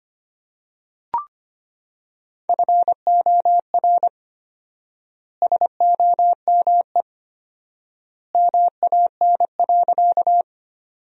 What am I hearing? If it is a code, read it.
Morse code, heard as a keyed tone, about 25 wpm, 700 Hz. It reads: FOR SOME MAN.
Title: 4x Word-Spacing - Sentences from Top 200 Words - 25wpm